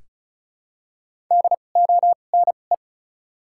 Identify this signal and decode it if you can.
Morse code, heard as a keyed tone, about 36 wpm, spaced as for 25 wpm, 700 Hz. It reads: DONE
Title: 2x Character Spacing - Top 500 Words - 35wpm